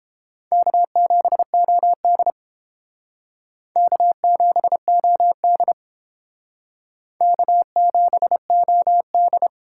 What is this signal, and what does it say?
Morse code, heard as a keyed tone, about 31 wpm, 700 Hz. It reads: K7OD K7OD K7OD